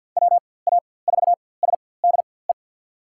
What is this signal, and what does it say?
Morse code, heard as a keyed tone, about 46 wpm, spaced as for 23 wpm, 700 Hz. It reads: WA4SDE
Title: USA Call Signs - 6 Characters - 4x Character Spacing - 50wpm